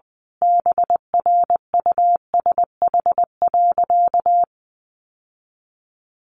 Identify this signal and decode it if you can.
Morse code, heard as a keyed tone, about 20 wpm, 700 Hz. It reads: BRUSH.
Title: Sentences from Top 1100 Words 20wpm